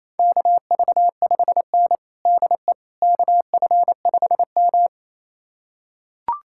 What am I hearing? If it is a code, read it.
Morse code, heard as a keyed tone, about 28 wpm, 700 Hz. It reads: KV5N DE KF5M
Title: USA Callsign DE Callsign - 4 Characters - 28wpm